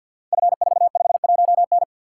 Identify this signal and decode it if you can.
Morse code, heard as a keyed tone, about 50 wpm, 700 Hz. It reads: F451N
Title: Code Groups - 5 Letter:Numbers - 50wpm